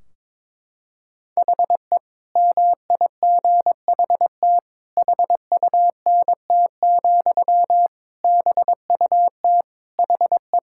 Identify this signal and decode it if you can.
Morse code, heard as a keyed tone, about 22 wpm, 700 Hz. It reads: HE MIGHT HUNT, BUT HE
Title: Sentences from Top 1600 Words 22wpm